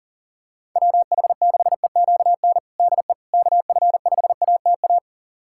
Morse code, heard as a keyed tone, about 40 wpm, 700 Hz. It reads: WH6EQN DE KF5ATA
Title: USA Callsign DE Callsign - 6 Characters - 40wpm